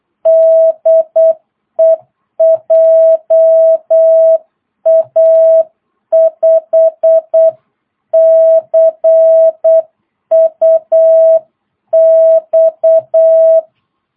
Morse code, heard as a keyed tone, about 8 wpm, 650 Hz. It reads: DE JA5CUX
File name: jh8cbh / 2022cw